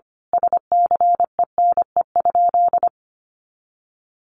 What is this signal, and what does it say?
Morse code, heard as a keyed tone, about 25 wpm, 700 Hz. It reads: SCENE?
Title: Sentences from Top 1100 Words 25wpm